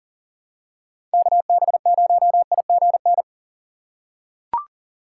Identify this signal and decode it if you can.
Morse code, heard as a keyed tone, about 40 wpm, 700 Hz. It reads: KB0IGN
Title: USA Call Signs - 6 Characters - 6x Character Spacing - 40wpm